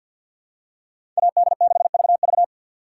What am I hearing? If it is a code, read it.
Morse code, heard as a keyed tone, about 51 wpm, 700 Hz. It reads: AD6VV
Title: USA Call Signs - 5 Characters - 6x Character Spacing - 50wpm